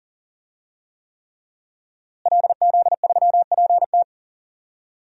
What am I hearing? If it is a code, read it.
Morse code, heard as a keyed tone, about 40 wpm, 700 Hz. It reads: LZ3PT
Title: DXCC-100 Call Signs - 5 Characters - Repeated 3x - 40wpm